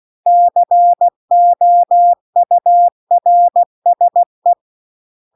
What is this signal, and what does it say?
Morse code, heard as a keyed tone, about 16 wpm, 700 Hz. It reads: COURSE
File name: CW-The-Big-Twoheaded-River-2-16WPM